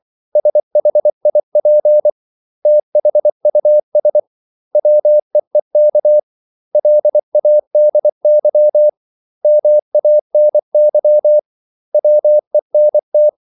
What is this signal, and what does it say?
Morse code, heard as a keyed tone, about 24 wpm, 600 Hz. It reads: SHIP THUS WEEK LADY MANY WENT
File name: CW / Practice / 500MostCommonEnglishWords24